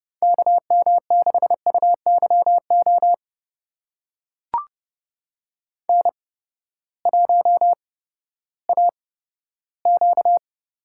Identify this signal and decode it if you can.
Morse code, heard as a keyed tone, about 31 wpm, spaced as for 15 wpm, 700 Hz. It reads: KM6UYO N1AQ
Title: USA Call Signs - 8x Character Spacing - 30wpm